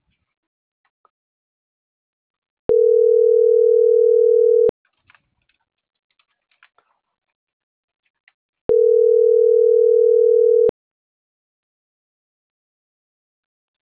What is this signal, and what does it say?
Ringback tone (North America), 440 + 480 Hz.
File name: CENTRAL MABE / out mariel.reyes